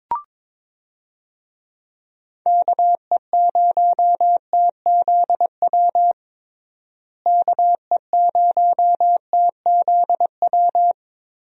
Morse code, heard as a keyed tone, about 22 wpm, 700 Hz. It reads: KE0TZW KE0TZW
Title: USA Call Signs - 6 Characters - Repeated 3x - 22wpm